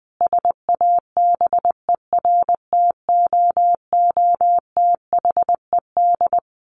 Morse code, heard as a keyed tone, about 20 wpm, 700 Hz. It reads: SABERTOOTHED
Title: A Sound Of Thunder 20wpm